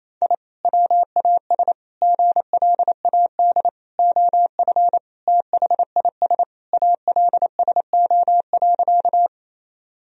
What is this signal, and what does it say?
Morse code, heard as a keyed tone, about 28 wpm, 700 Hz. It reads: I WAS GLAD OF THIS ALSO.
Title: Sentences from Top 1000 Words - Review Part 1 - 28wpm